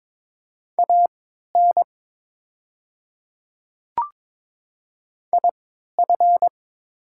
Morse code, heard as a keyed tone, about 22 wpm, spaced as for 12 wpm, 700 Hz. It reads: AN IF
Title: 3x Character Spacing - Top 100 Words - 22wpm